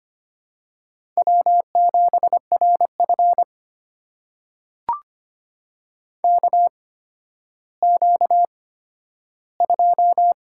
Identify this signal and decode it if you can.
Morse code, heard as a keyed tone, about 25 wpm, spaced as for 12 wpm, 700 Hz. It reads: W7RF KQ2
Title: USA Call Signs - 8x Character Spacing - 25wpm